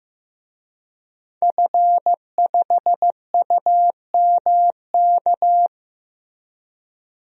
Morse code, heard as a keyed tone, about 15 wpm, 700 Hz. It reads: F5UMK
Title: DXCC-100 Call Signs - 5 Characters - Repeated 2x - 15wpm